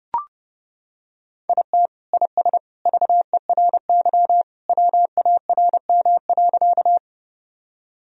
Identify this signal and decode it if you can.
Morse code, heard as a keyed tone, about 30 wpm, 700 Hz. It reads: IT IS VERY WARM.
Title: Sentences from Top 1000 Words - Review Part 2 - 30wpm